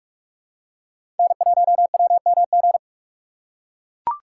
Morse code, heard as a keyed tone, about 44 wpm, 700 Hz. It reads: N1WMG
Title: USA Call Signs - 5 Characters - 6x Character Spacing - 45wpm